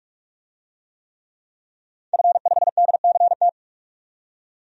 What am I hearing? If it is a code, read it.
Morse code, heard as a keyed tone, about 46 wpm, 700 Hz. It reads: F5DCT
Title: DXCC-100 Call Signs - 5 Characters - 45wpm